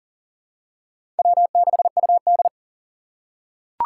Morse code, heard as a keyed tone, about 40 wpm, 700 Hz. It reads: W6UD
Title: USA Call Signs - 4 Characters - 6x Character Spacing - 40wpm